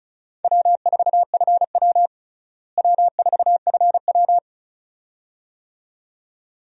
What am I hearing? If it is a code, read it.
Morse code, heard as a keyed tone, about 36 wpm, 700 Hz. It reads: W4FW W4FW
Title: USA Call Signs - 4 Characters - Repeated 2x - 35wpm